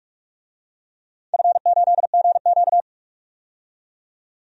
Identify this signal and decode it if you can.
Morse code, heard as a keyed tone, about 48 wpm, 700 Hz. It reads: F8GQ
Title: DXCC-100 Call Signs - 4 Characters - 45wpm